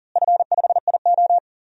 Morse code, heard as a keyed tone, about 38 wpm, 700 Hz. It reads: F5IO
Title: DXCC-100 Call Signs - 4 Characters - Repeated 3x - 40wpm